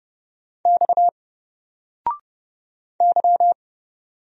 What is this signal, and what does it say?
Morse code, heard as a keyed tone, about 30 wpm, 700 Hz. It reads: X Y
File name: Single-Letter-Number-30wpm